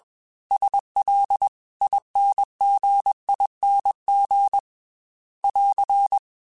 Morse code, heard as a keyed tone, about 21 wpm, 800 Hz. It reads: SL INGING <AR>
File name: Special INGs / Special INGs-mp3s / Dble Ings sp @20